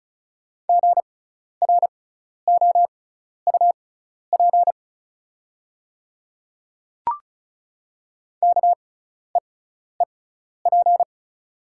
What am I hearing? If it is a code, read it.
Morse code, heard as a keyed tone, about 36 wpm, spaced as for 12 wpm, 690 Hz. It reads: GROUP KEEP
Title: 6x Character Spacing - Top 300 Words - 35wpm